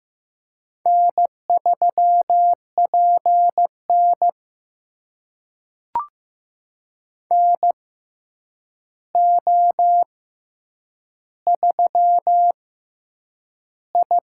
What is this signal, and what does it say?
Morse code, heard as a keyed tone, about 15 wpm, spaced as for 8 wpm, 700 Hz. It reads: N3PN NO3I
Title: USA Call Signs - 4 Characters - 6x Character Spacing - 15wpm